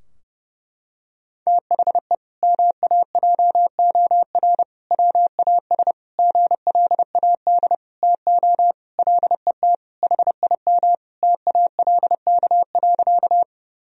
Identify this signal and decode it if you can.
Morse code, heard as a keyed tone, about 30 wpm, 700 Hz. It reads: THE MAJOR WAS GLAD TO LET HIM TALK.
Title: Sentences from Top 1000 Words - Review Part 4 - 30wpm